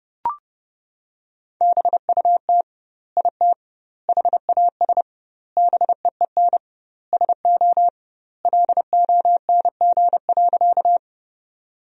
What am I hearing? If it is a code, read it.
Morse code, heard as a keyed tone, about 32 wpm, 700 Hz. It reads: BUT IT HAS BEEN SO LONG.
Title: 2x Word-Spacing - Sentences from Top 100 Words - 30wpm